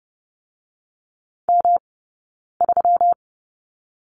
Morse code, heard as a keyed tone, about 30 wpm, 700 Hz. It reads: M 3
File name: Lesson 20d - Number 3 - Two Characters 30wpm